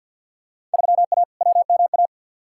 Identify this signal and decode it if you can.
Morse code, heard as a keyed tone, about 50 wpm, 700 Hz. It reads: 3A WMA
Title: Field Day Exchanges - 50wpm